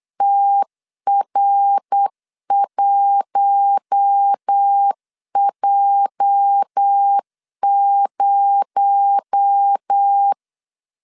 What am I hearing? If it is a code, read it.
Morse code, heard as a keyed tone, about 8 wpm, 800 Hz. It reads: TR1J0
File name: LEZIONI / 8 YJ45 / 8 YJ 8wpm 3313